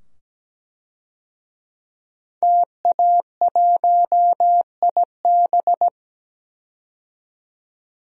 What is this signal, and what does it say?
Morse code, heard as a keyed tone, about 17 wpm, 700 Hz. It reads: TA1IB
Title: DXCC-100 Call Signs - 5 Characters - 17wpm